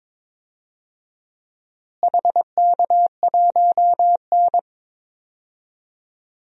Morse code, heard as a keyed tone, about 22 wpm, 700 Hz. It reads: HK1N
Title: DXCC-100 Call Signs - 4 Characters - Repeated 3x - 22wpm